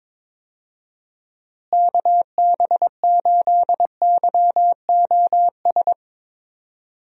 Morse code, heard as a keyed tone, about 21 wpm, 700 Hz. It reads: KB8YOS
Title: USA Call Signs - 6 Characters - Repeated 2x - 22wpm